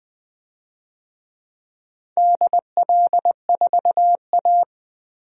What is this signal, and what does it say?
Morse code, heard as a keyed tone, about 20 wpm, 700 Hz. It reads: DL4A
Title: DXCC-100 Call Signs - 4 Characters - Repeated 2x - 20wpm